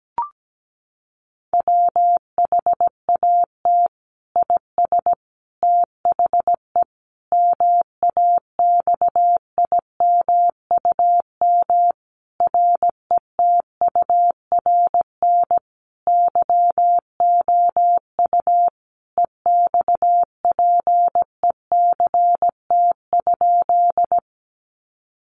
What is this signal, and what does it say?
Morse code, heard as a keyed tone, about 17 wpm, 700 Hz. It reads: WHAT IS THE MAXIMUM RETURN YOU EXPECT?
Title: Sentences from Top 1300 Words 17wpm